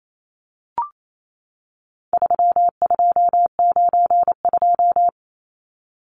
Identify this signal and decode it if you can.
Morse code, heard as a keyed tone, about 28 wpm, 700 Hz. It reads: 3292
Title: Numbers - 4-Digit 28wpm